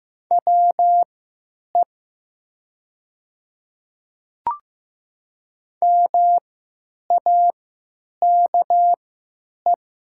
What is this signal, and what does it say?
Morse code, heard as a keyed tone, about 15 wpm, spaced as for 9 wpm, 700 Hz. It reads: WE MAKE
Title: 3x Character Spacing - Top 100 Words - 15wpm